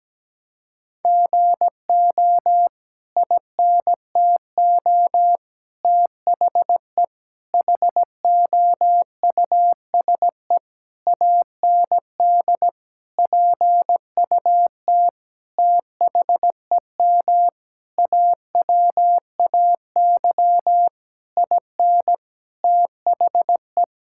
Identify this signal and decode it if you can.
Morse code, heard as a keyed tone, about 17 wpm, 700 Hz. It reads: GO INTO THE HOUSE AND PUT THEM AWAY IN THE
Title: Sentences from Top 1000 Words - Review Part 5 - 17wpm